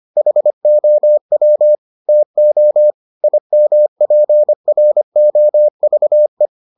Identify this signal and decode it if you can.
Morse code, heard as a keyed tone, about 25 wpm, 600 Hz. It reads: HOW TO IMPROVE